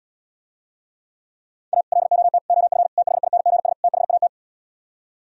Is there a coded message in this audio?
Yes: Morse code from 1.7 to 4.3 s, heard as a keyed tone, about 77 wpm, 700 Hz. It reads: A LOT OF THINGS THAT